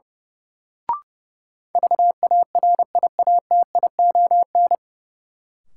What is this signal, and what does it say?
Morse code, heard as a keyed tone, about 30 wpm, 700 Hz. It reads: VARIATION
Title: Sentences from Top 1600 Words - New Word Drill - 30wpm